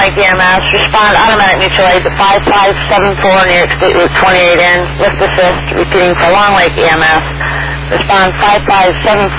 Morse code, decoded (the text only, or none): none